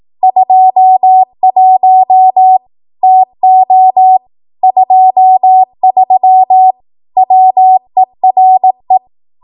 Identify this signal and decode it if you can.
Morse code, heard as a keyed tone, about 18 wpm, 750 Hz. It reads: 21 TO 23 WERE
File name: Archive / 18 WPM / 18WPM